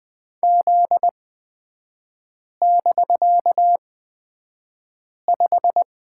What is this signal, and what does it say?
Morse code, heard as a keyed tone, about 20 wpm, spaced as for 5 wpm, 700 Hz. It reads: Z<BK>5
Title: 250 Lesson 40e - BK - Three Characters 20wpm